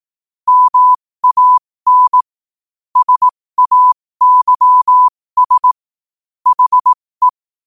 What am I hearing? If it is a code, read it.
Morse code, heard as a keyed tone, about 18 wpm, spaced as for 15 wpm, 1000 Hz. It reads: MAN SAYS HE
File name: qotdfiles / 15wpm